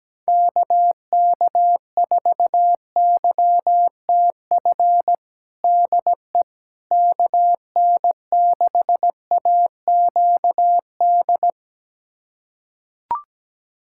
Morse code, heard as a keyed tone, about 17 wpm, 700 Hz. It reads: KK4YTF DE KN6AQD
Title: USA Callsign DE Callsign - 6 Characters - 17wpm